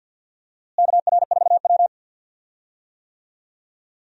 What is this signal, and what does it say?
Morse code, heard as a keyed tone, about 52 wpm, 700 Hz. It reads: KR4W
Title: USA Call Signs - 4 Characters - Repeated 3x - 50wpm